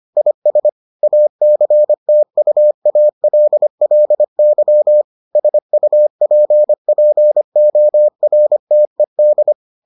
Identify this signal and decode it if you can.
Morse code, heard as a keyed tone, about 25 wpm, 600 Hz. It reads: IS ACTUALLY SUPPORTED